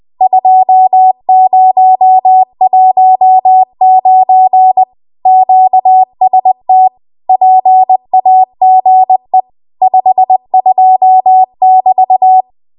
Morse code, heard as a keyed tone, about 20 wpm, 750 Hz. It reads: 2019 QST PAGE 52 <BT>
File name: Archive / 20 WPM / 20WPM